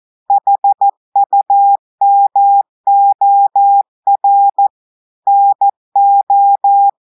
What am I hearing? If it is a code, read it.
Morse code, heard as a keyed tone, about 14 wpm, 800 Hz. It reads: HUMOR NO
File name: audios / portugais / portugais texte en clair 2 vitesse 14 mots minutes